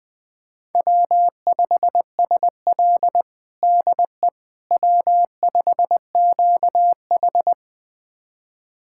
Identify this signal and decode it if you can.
Morse code, heard as a keyed tone, about 21 wpm, 700 Hz. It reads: W5SL DE W5QH
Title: USA Callsign DE Callsign - 4 Characters - 20wpm